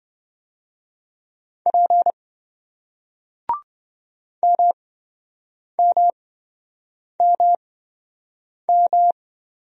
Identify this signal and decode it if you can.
Morse code, heard as a keyed tone, about 28 wpm, spaced as for 7 wpm, 700 Hz. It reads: P MMMM